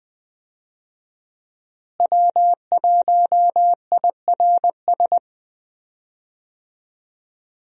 Morse code, heard as a keyed tone, about 20 wpm, 700 Hz. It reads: W1IRS
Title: USA Call Signs - 5 Characters - Repeated 2x - 20wpm